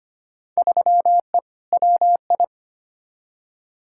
Morse code, heard as a keyed tone, about 25 wpm, 700 Hz. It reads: 3E WI
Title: Field Day Exchanges Encore - 25wpm